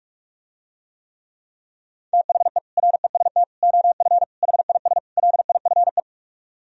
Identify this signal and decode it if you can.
Morse code, heard as a keyed tone, about 44 wpm, 700 Hz. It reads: THE REST OF HIS LIFE